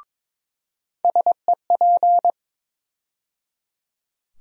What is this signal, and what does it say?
Morse code, heard as a keyed tone, about 22 wpm, 700 Hz. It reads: SEP
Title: USA Call Sign Postfixes - 3 Letters 22wpm